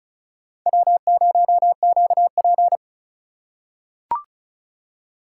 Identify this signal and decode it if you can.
Morse code, heard as a keyed tone, about 36 wpm, 700 Hz. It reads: W0QP
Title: USA Call Signs - 4 Characters - 4x Character Spacing - 35wpm